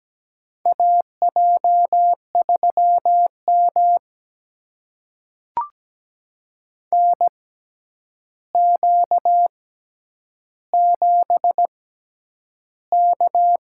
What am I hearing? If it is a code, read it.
Morse code, heard as a keyed tone, about 17 wpm, spaced as for 9 wpm, 700 Hz. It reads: AJ3M NQ7K
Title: USA Call Signs - 4 Characters - 6x Character Spacing - 17wpm